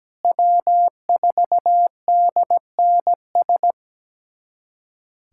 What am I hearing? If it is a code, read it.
Morse code, heard as a keyed tone, about 17 wpm, 700 Hz. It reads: W4DNS
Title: USA Callsign DE Callsign - 5 Characters - 17wpm